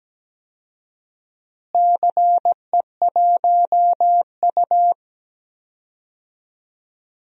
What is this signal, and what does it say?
Morse code, heard as a keyed tone, about 17 wpm, 700 Hz. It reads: CE1U
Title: DXCC-100 Call Signs - 4 Characters - Repeated 3x - 17wpm